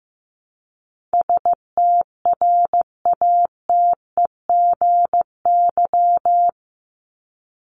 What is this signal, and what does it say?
Morse code, heard as a keyed tone, about 15 wpm, 700 Hz. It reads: STRATEGY
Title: Top 501-750 Words 15wpm